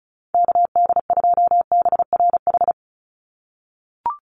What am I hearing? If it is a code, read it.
Morse code, heard as a keyed tone, about 34 wpm, 700 Hz. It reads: KD2BRH
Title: USA Call Signs 35wpm